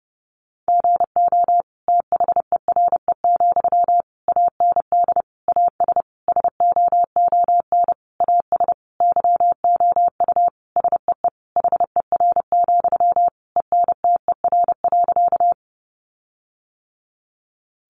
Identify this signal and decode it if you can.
Morse code, heard as a keyed tone, about 30 wpm, 700 Hz. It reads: GO THERE, AND AS SOON AS YOU SEE HER, ENTER.
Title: Sentences from Top 1300 Words 30wpm